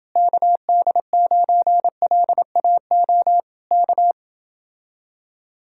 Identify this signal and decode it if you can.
Morse code, heard as a keyed tone, about 28 wpm, 700 Hz. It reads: KD9LAO K